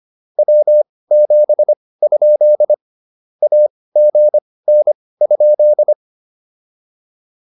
Morse code, heard as a keyed tone, about 25 wpm, spaced as for 18 wpm, 600 Hz. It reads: W7? AGN?